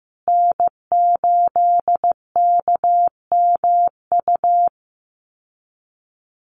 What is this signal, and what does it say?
Morse code, heard as a keyed tone, about 15 wpm, 700 Hz. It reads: N8KMU
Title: USA Call Signs Repeated 2x 15wpm